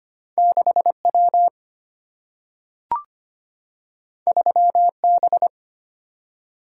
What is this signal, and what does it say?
Morse code, heard as a keyed tone, about 25 wpm, 700 Hz. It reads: 6 W 3 B
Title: Single-Letter Single-Number 25wpm